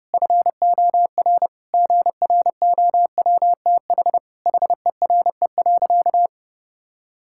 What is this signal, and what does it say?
Morse code, heard as a keyed tone, about 31 wpm, 700 Hz. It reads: FOR GROWTH HERE.
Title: Sentences from Top 1000 Words - Review Part 5 - 30wpm